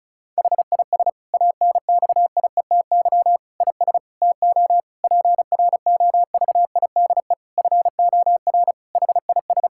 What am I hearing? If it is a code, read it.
Morse code, heard as a keyed tone, about 34 wpm, 700 Hz. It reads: HIS ANXIETY IS TO PROVIDE FOR HIS
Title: Sentences from Top 1600 Words 35wpm